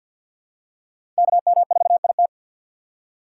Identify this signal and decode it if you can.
Morse code, heard as a keyed tone, about 53 wpm, 700 Hz. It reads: KM4IT
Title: USA Call Signs - 5 Characters - 6x Character Spacing - 50wpm